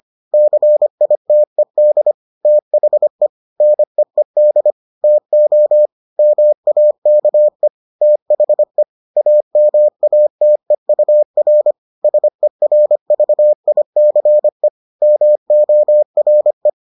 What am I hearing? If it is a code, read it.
Morse code, heard as a keyed tone, about 25 wpm, 600 Hz. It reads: CITED THE NEED TO MAKE THE AMATEUR SERVICE MORE